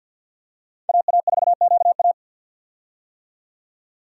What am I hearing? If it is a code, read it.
Morse code, heard as a keyed tone, about 46 wpm, 700 Hz. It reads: AA4QA
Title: USA Call Signs - 5 Characters - 50wpm